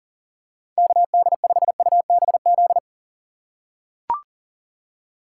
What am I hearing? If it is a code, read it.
Morse code, heard as a keyed tone, about 40 wpm, 700 Hz. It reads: KD5UBZ
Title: USA Call Signs - 3x Character Spacing - 40wpm